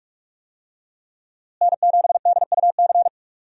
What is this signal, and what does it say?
Morse code, heard as a keyed tone, about 47 wpm, 700 Hz. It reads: N7DUC
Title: USA Call Signs - 5 Characters - Repeated 3x - 45wpm